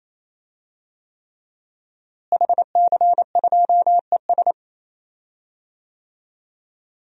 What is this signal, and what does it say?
Morse code, heard as a keyed tone, about 27 wpm, 700 Hz. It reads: HC2ES
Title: DXCC-100 Call Signs - 5 Characters - Repeated 3x - 28wpm